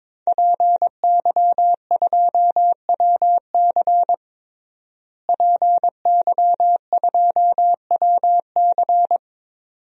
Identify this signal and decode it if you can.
Morse code, heard as a keyed tone, about 22 wpm, 700 Hz. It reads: PY2WC PY2WC
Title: DXCC-100 Call Signs - 5 Characters - Repeated 2x - 22wpm